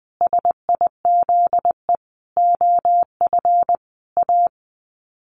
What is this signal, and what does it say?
Morse code, heard as a keyed tone, about 20 wpm, 700 Hz. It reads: SIZE OF A
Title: Sets of 5 Words 20wpm